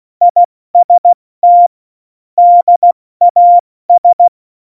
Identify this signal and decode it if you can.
Morse code, heard as a keyed tone, about 16 wpm, 700 Hz. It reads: IST DAS
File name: Morsekurs / Witze / Witz 02. Januar 2018, 70 F80